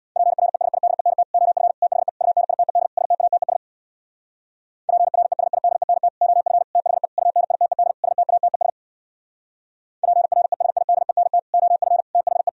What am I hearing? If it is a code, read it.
Morse code, heard as a keyed tone, about 61 wpm, 700 Hz. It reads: PRESIDENT OF THE UNITED STATES PRESIDENT OF THE UNITED STATES PRESIDENT OF THE
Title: Speed-Racing - Sets of 5 Words 50wpm